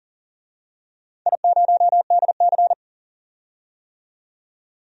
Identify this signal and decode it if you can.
Morse code, heard as a keyed tone, about 40 wpm, 700 Hz. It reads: I0DC
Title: DXCC-100 Call Signs - 4 Characters - 40wpm